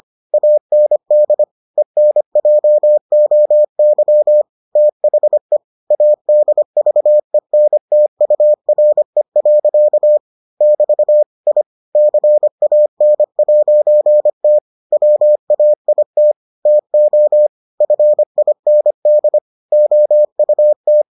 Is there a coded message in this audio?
Yes: Morse code from 0.3 to 21.0 s, heard as a keyed tone, about 25 wpm, 600 Hz. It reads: AND ENJOY THE ADVENTURE. <BT> I CAN'T WAIT TO FIND OUT